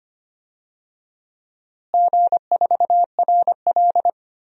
Morse code, heard as a keyed tone, about 25 wpm, 700 Hz. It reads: G4RL
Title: DXCC-100 Call Signs - 4 Characters - Repeated 3x - 25wpm